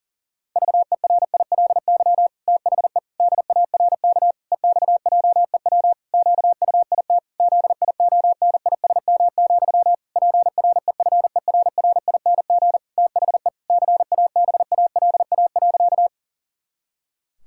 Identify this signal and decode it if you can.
Morse code, heard as a keyed tone, about 40 wpm, 700 Hz. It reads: VERILY THE DARK EXJEW QUIT ZIONISM, PREFERRING THE CABALA.
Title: Pangram Challenge v2 - 40wpm